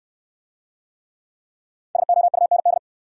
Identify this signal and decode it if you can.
Morse code, heard as a keyed tone, about 65 wpm, 700 Hz. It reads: SOUND